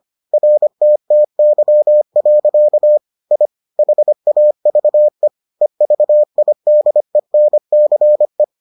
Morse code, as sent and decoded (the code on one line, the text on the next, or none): .-. - - -.-- .-.-.- / .. / .... .- ...- . / . ...- .. -.. -. -.-. .
RTTY. I HAVE EVIDENCE